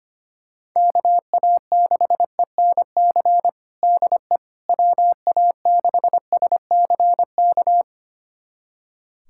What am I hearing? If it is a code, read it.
Morse code, heard as a keyed tone, about 25 wpm, 700 Hz. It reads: KA6ENC DE WA6SCK
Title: USA Callsign DE Callsign - 6 Characters - 25wpm